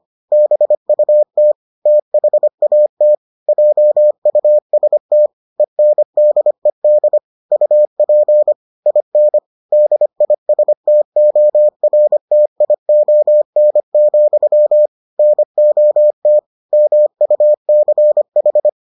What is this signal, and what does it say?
Morse code, heard as a keyed tone, about 25 wpm, 600 Hz. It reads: BUT THAT JUST ENDED UP IN DISTORTION, NOT MUCH